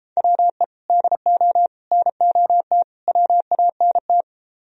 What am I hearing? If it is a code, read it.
Morse code, heard as a keyed tone, about 30 wpm, 700 Hz. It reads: WE DO NOT WANT
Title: Speed-Racing - Sets of 4 Words 28wpm